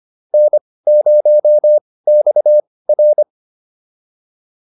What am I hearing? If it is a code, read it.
Morse code, heard as a keyed tone, about 25 wpm, spaced as for 18 wpm, 600 Hz. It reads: N0XR
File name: POTA203 18